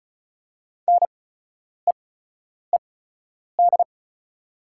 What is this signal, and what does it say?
Morse code, heard as a keyed tone, about 36 wpm, spaced as for 10 wpm, 690 Hz. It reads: NEED
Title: 8x Character Spacing - Top 200 Words - 35wpm